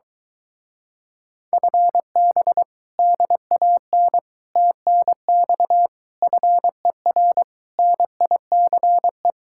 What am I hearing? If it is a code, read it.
Morse code, heard as a keyed tone, about 22 wpm, 700 Hz. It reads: FB DAN TNX FER NICE